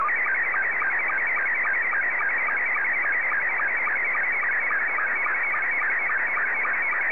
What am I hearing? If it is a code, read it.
SSTV (slow-scan television), Scottie 2.